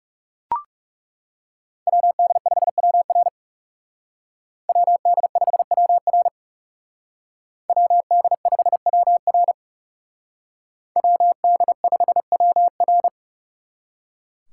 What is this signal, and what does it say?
Morse code, heard as a keyed tone, about 37 wpm, 700 Hz. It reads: WD5WR WD5WR WD5WR WD5WR
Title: Speed Racing - USA 2x2 Call Signs 30wpm